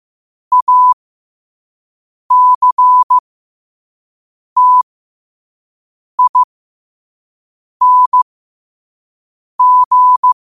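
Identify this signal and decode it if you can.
Morse code, heard as a keyed tone, about 15 wpm, spaced as for 5 wpm, 1000 Hz. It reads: ACTING
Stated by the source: Morse code practice at 5 wpm.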